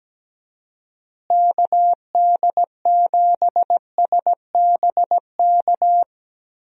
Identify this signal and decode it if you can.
Morse code, heard as a keyed tone, about 17 wpm, 700 Hz. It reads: KD7SBK
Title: USA Call Signs - 6 Characters - Repeated 3x - 17wpm